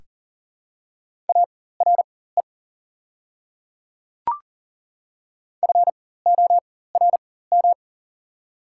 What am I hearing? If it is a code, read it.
Morse code, heard as a keyed tone, about 40 wpm, spaced as for 19 wpm, 690 Hz. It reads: ARE FORM
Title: 4x Character Spacing - Top 200 Words - 40wpm